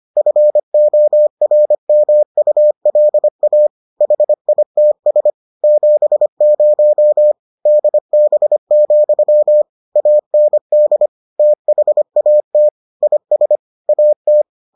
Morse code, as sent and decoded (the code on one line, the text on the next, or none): ..-. --- .-. -- ..- .-.. .- / .... .. - ... / --... ----- / -.. -... --..-- / .- -. -.. / - .... .- - / .. ... / .- -
FORMULA HITS 70 DB, AND THAT IS AT